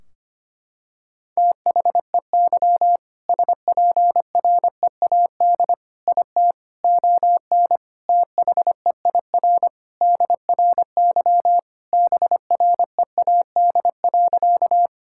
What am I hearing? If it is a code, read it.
Morse code, heard as a keyed tone, about 25 wpm, 700 Hz. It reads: THEY SPREAD IT ON THEIR DRY BREAD.
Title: Sentences from Top 1600 Words 25wpm